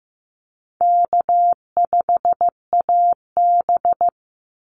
Morse code, heard as a keyed tone, about 15 wpm, 700 Hz. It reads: K5AB
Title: Easy Call Signs Repeated 2x - 15wpm